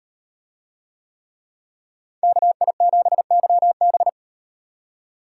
Morse code, heard as a keyed tone, about 38 wpm, 700 Hz. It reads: KI7YB